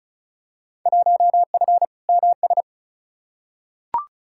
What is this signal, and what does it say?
Morse code, heard as a keyed tone, about 36 wpm, 700 Hz. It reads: 1F MS
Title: Field Day Exchanges Encore - 35wpm